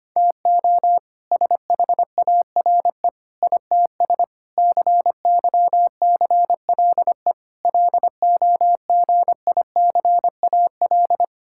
Morse code, heard as a keyed tone, about 25 wpm, 700 Hz. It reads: TO SHARE ITS CYCLE LOGICAL
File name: practice / 25wpm / All-Age Appropriate Jokes - 25wpm